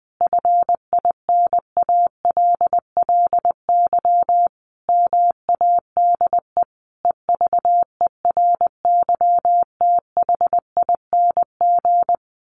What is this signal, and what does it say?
Morse code, heard as a keyed tone, about 20 wpm, 700 Hz. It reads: FINALLY MADE EVERYTHING
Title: Sentences from Top 1300 Words 20wpm